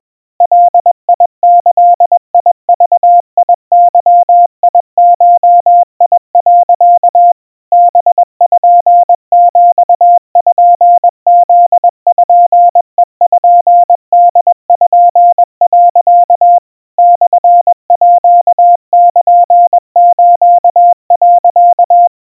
Morse code (Japanese Wabun code), heard as a keyed tone, about 21 wpm, 700 Hz. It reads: ガギグゲゴ、 パピプペポ、 モールス、
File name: kawa / Morse sample / iroha 70 cpm 700 Hz Crispy